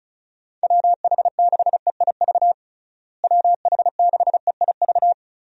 Morse code, heard as a keyed tone, about 36 wpm, 700 Hz. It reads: WH6EIV WH6EIV
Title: USA Call Signs - 6 Characters - Repeated 3x - 35wpm